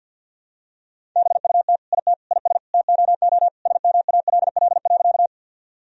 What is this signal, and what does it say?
Morse code, heard as a keyed tone, about 46 wpm, 700 Hz. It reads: BUT IT IS TOO SMALL.
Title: Sentences from Top 1000 Words - Review Part 1 - 50wpm